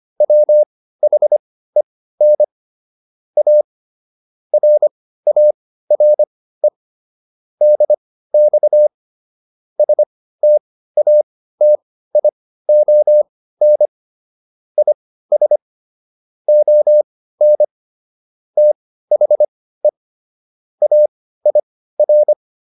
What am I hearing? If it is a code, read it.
Morse code, heard as a keyed tone, about 25 wpm, spaced as for 15 wpm, 600 Hz. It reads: WHEN A RARE DX STATION IS ON THE AIR